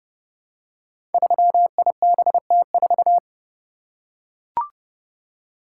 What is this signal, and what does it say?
Morse code, heard as a keyed tone, about 30 wpm, 700 Hz. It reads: 3IBT4
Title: Code Groups - 5 Letter:Numbers - 30wpm